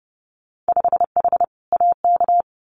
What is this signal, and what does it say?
Morse code, heard as a keyed tone, about 30 wpm, 700 Hz. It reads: 5H AK
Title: Winter Field Day Exchanges Encore 30wpm